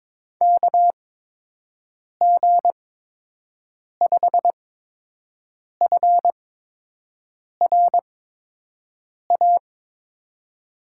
Morse code, heard as a keyed tone, about 22 wpm, spaced as for 6 wpm, 700 Hz. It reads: KG5FRA
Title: USA Call Signs - 8x Character Spacing - 22wpm